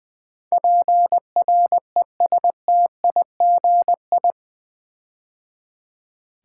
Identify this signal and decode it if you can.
Morse code, heard as a keyed tone, about 20 wpm, 700 Hz. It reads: PRESTIGI
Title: Build A Word - 4 Letters At A Time - 11-Letter Words - 20wpm